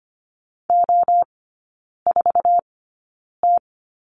Morse code, heard as a keyed tone, about 25 wpm, spaced as for 9 wpm, 700 Hz. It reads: O4T
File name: course / 25:10wpm / 100 Lesson 17e - Letter u - Three Characters 25wpm